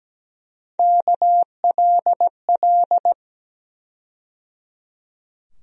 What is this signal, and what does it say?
Morse code, heard as a keyed tone, about 17 wpm, 700 Hz. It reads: KLL
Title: USA Call Sign Postfixes - 3 Letters 17wpm